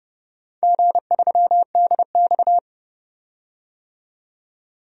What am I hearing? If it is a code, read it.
Morse code, heard as a keyed tone, about 30 wpm, 700 Hz. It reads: G3DX
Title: DXCC-100 Call Signs - 4 Characters - 30wpm